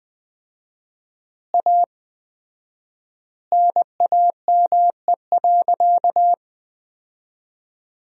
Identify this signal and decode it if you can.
Morse code, heard as a keyed tone, about 20 wpm, 700 Hz. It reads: A NAME.
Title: 4x Word-Spacing - Sentences from Top 500 Words - 20wpm